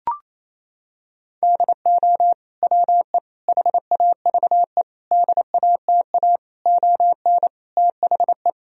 Morse code, heard as a keyed tone, about 27 wpm, 700 Hz. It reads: DO WE HAVE DATA ON THE
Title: Sentences from Top 1000 Words - Review Part 3 - 28wpm